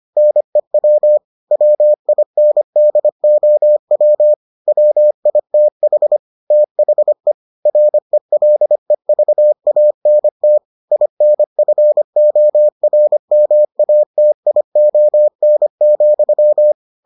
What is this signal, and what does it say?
Morse code, heard as a keyed tone, about 25 wpm, 600 Hz. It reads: NEW WINDOW WITH THE RELEVANT INFORMATION,